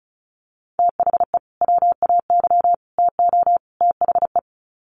Morse code, heard as a keyed tone, about 34 wpm, 700 Hz. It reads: THE WAY TO THE
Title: Sets of 4 Words 35wpm